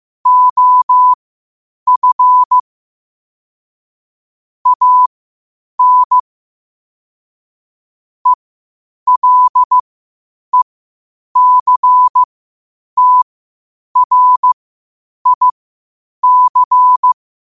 Morse code, heard as a keyed tone, about 15 wpm, spaced as for 8 wpm, 1000 Hz. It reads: OF AN ELECTRIC
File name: qotdfiles / 8wpm